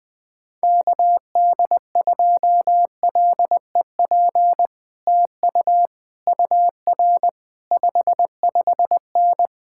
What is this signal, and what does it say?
Morse code, heard as a keyed tone, about 20 wpm, 700 Hz. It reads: KD2LEP TU UR 55N
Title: Speed-Racing Common QSO Phrases Encore - 17wpm